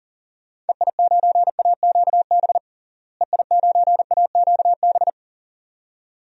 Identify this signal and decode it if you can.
Morse code, heard as a keyed tone, about 40 wpm, 700 Hz. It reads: EI9AQB EI9AQB
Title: DXCC-100 Call Signs - 6 Characters - Repeated 2x - 40wpm